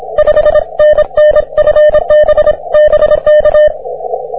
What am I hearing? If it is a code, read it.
Morse code, heard as a keyed tone, about 26 wpm, 620 Hz. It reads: 5NN FB BK